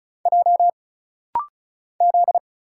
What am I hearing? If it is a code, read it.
Morse code, heard as a keyed tone, about 34 wpm, 700 Hz. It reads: J Z
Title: Single Letters - Rapid-Fire 35wpm